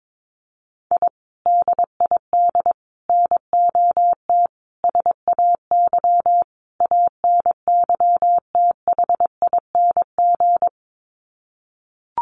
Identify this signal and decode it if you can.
Morse code, heard as a keyed tone, about 22 wpm, 700 Hz. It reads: I DID NOT SAY ANYTHING
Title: Sets of 5 Words Encore2 22wpm